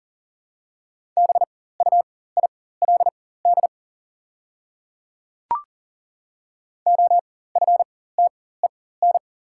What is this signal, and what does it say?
Morse code, heard as a keyed tone, about 40 wpm, spaced as for 19 wpm, 700 Hz. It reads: BUILD OFTEN
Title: 4x Character Spacing - Top 300 Words - 40wpm